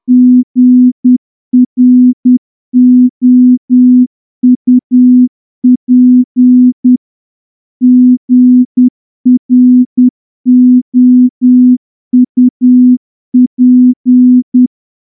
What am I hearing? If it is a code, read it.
Morse code, heard as a keyed tone, about 10 wpm, 250 Hz. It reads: GROUP GROUP
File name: morse-training-10WPM